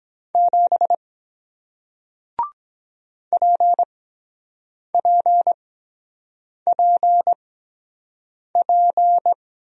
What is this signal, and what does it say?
Morse code, heard as a keyed tone, about 23 wpm, spaced as for 7 wpm, 700 Hz. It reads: 7 PPPP